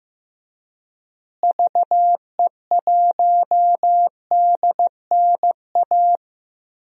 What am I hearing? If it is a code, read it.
Morse code, heard as a keyed tone, about 15 wpm, 700 Hz. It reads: VE1DNA
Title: DXCC-100 Call Signs - 6 Characters - Repeated 3x - 15wpm